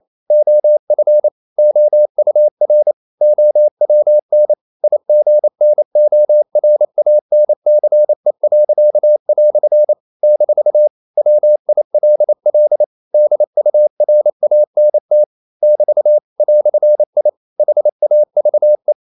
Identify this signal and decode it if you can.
Morse code, heard as a keyed tone, about 28 wpm, 600 Hz. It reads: OF OUR OWN IGNORANCE." - WILL DURANT <BT> "I HAVE
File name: CW / Practice / EnglishQuotations0228